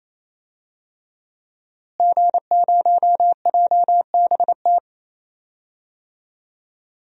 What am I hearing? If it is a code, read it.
Morse code, heard as a keyed tone, about 28 wpm, 700 Hz. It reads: G0JBT
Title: DXCC-100 Call Signs - 5 Characters - Repeated 3x - 28wpm